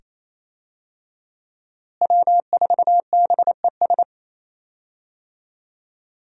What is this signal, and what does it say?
Morse code, heard as a keyed tone, about 27 wpm, 700 Hz. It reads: W4BES
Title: USA Call Signs - 5 Characters - Repeated 2x - 28wpm